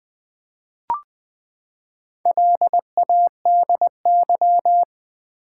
Morse code, heard as a keyed tone, about 21 wpm, 700 Hz. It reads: LADY
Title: Four Letter Words - 20wpm